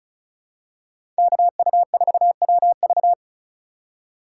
Morse code, heard as a keyed tone, about 34 wpm, 700 Hz. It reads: KU4WV
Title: USA Call Signs - 5 Characters - 8x Character Spacing - 35wpm